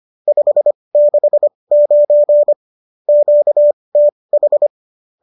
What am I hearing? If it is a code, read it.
Morse code, heard as a keyed tone, about 25 wpm, spaced as for 20 wpm, 590 Hz. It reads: 569 QTH
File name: qso208 20